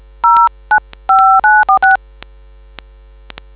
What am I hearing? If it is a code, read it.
DTMF, heard as telephone touch tones, tones of about 190 ms. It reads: 095946